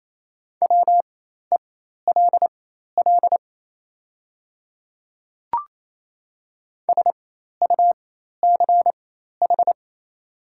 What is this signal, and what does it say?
Morse code, heard as a keyed tone, about 28 wpm, spaced as for 13 wpm, 700 Hz. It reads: WELL SUCH
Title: 4x Character Spacing - Top 200 Words - 28wpm